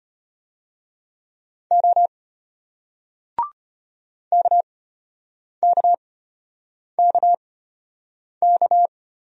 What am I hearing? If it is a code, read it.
Morse code, heard as a keyed tone, about 33 wpm, spaced as for 8 wpm, 700 Hz. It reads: O KKKK